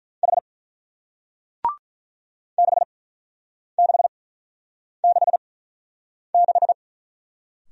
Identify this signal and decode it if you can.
Morse code, heard as a keyed tone, about 45 wpm, spaced as for 9 wpm, 700 Hz. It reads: H 6666